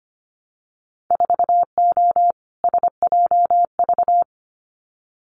Morse code, heard as a keyed tone, about 25 wpm, 700 Hz. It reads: 4O SJV
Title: Winter Field Day Exchanges Encore 25wpm